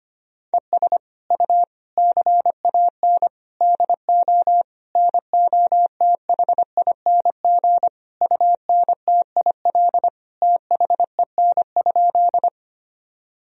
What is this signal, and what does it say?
Morse code, heard as a keyed tone, about 25 wpm, 700 Hz. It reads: ES U CAN DO NOTHING UNTIL THEN?
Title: Sentences from Top 500 Words and Abbreviations 25wpm